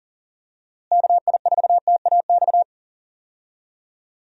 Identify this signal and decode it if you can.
Morse code, heard as a keyed tone, about 40 wpm, 700 Hz. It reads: KI4TAX
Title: USA Call Signs - 6 Characters - Repeated 3x - 40wpm